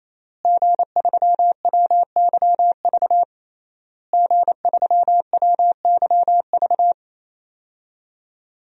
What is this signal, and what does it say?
Morse code, heard as a keyed tone, about 28 wpm, 700 Hz. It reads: G3WYV G3WYV
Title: DXCC-100 Call Signs - 5 Characters - Repeated 3x - 28wpm